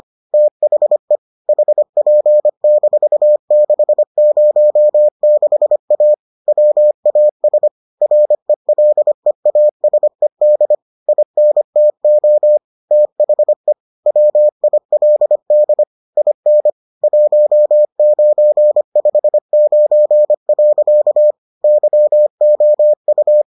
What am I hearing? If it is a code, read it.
Morse code, heard as a keyed tone, about 25 wpm, 600 Hz. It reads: THE HP-606A WAS RELEASED INTO THE WILD IN 1959. YOU